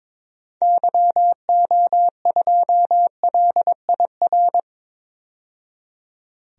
Morse code, heard as a keyed tone, about 21 wpm, 700 Hz. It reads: YO2LIR
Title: DXCC-100 Call Signs - 6 Characters - 22wpm